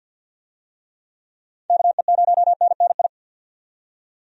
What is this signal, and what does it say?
Morse code, heard as a keyed tone, about 47 wpm, 700 Hz. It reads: KE0NNI
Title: USA Call Signs - 6 Characters - 50wpm